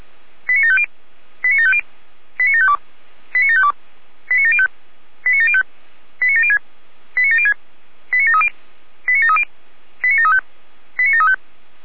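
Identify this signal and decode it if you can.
Five-tone selective call (ZVEI), heard as quick runs of five tones, tones of about 70 ms. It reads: 89750 89750 89752 89752 89705 89705 89706 89706 89730 89730 89735 89735
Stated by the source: ZVEI five-tone call received on an SDR.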